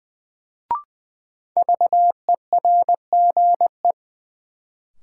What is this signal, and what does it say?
Morse code, heard as a keyed tone, about 20 wpm, 700 Hz. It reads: VERGE
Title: Sets of 4 Words - New Word Drill - 20wpm